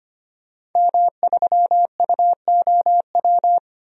Morse code, heard as a keyed tone, about 25 wpm, 700 Hz. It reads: M3UOW
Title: DXCC-100 Call Signs - 5 Characters - 25wpm